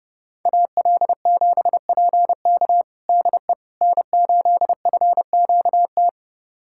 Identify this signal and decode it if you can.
Morse code, heard as a keyed tone, about 30 wpm, 700 Hz. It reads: AL7PK DE N8FQT
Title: USA Callsign DE Callsign - 5 Characters - 30wpm